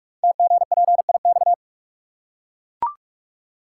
Morse code, heard as a keyed tone, about 44 wpm, 700 Hz. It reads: TGPIX
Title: Code Groups - 5 Letters - 45wpm